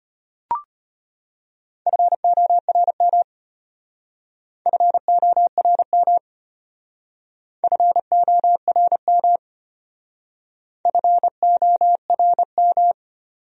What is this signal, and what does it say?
Morse code, heard as a keyed tone, about 31 wpm, 700 Hz. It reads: FORM FORM FORM FORM